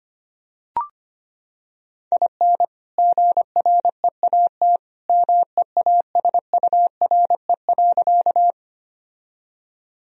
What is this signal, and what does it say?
Morse code, heard as a keyed tone, about 25 wpm, 700 Hz. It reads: IN GREAT MEASURE.
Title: Sentences from Top 1000 Words - Review Part 2 - 25wpm